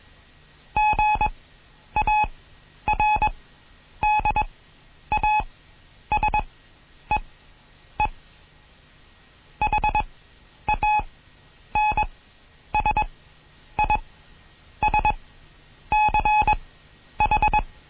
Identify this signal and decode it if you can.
Morse code, heard as a keyed tone, about 21 wpm, spaced as for 9 wpm, 850 Hz. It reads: GARDASEE HANSISCH
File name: Morsen / Kurs A 20